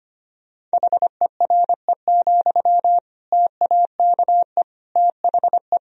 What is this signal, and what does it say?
Morse code, heard as a keyed tone, about 25 wpm, 700 Hz. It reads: HERE, TAKE THE
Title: Sentences from Top 1000 Words - Review Part 1 - 25wpm